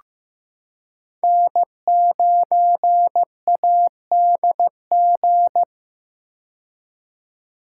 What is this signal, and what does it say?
Morse code, heard as a keyed tone, about 15 wpm, 700 Hz. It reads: N9ADG
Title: K1USN SST QSO Practice - 15wpm